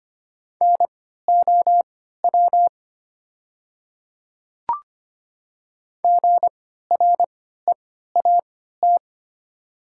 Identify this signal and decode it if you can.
Morse code, heard as a keyed tone, about 25 wpm, spaced as for 14 wpm, 700 Hz. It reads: NOW GREAT
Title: 3x Character Spacing - Top 500 Words - 25wpm